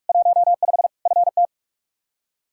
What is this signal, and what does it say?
Morse code, heard as a keyed tone, about 45 wpm, 700 Hz. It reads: 15 FT
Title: Practice QSO - Exchange 2 - Info - 45wpm